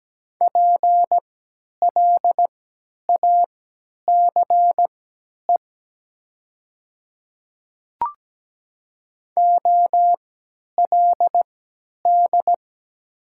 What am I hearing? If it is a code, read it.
Morse code, heard as a keyed tone, about 17 wpm, spaced as for 10 wpm, 700 Hz. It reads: PLACE OLD
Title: 3x Character Spacing - Top 200 Words - 17wpm